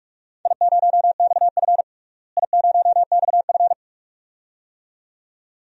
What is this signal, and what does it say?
Morse code, heard as a keyed tone, about 43 wpm, 700 Hz. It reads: I0XF I0XF
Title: DXCC-100 Call Signs - 4 Characters - Repeated 2x - 45wpm